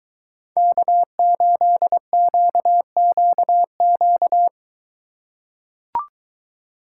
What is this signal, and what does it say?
Morse code, heard as a keyed tone, about 23 wpm, 700 Hz. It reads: K8QQQ